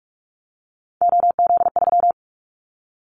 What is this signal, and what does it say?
Morse code, heard as a keyed tone, about 46 wpm, 700 Hz. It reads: OZ3
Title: Callsign Prefixes for Top 50 DXCC - 45wpm